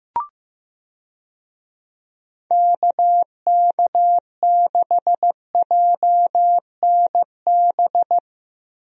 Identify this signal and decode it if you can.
Morse code, heard as a keyed tone, about 15 wpm, 700 Hz. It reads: KK6JNB
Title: USA Call Signs - 6 Characters - Repeated 3x - 15wpm